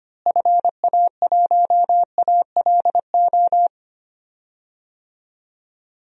Morse code, heard as a keyed tone, about 25 wpm, 700 Hz. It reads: FA1ALO
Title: DXCC-100 Call Signs - 6 Characters - Repeated 2x - 25wpm